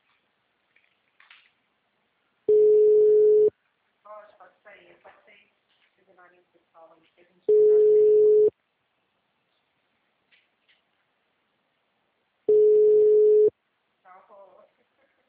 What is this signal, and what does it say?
Ringback tone (Europe, 425 Hz), with speech.